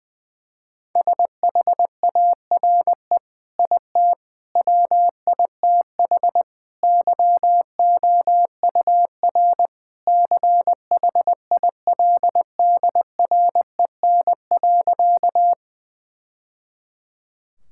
Morse code, heard as a keyed tone, about 21 wpm, 700 Hz. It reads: SHARE IT WITH YOUR CHILDREN.
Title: Sentences from Top 1000 Words - Review Part 6 - 20wpm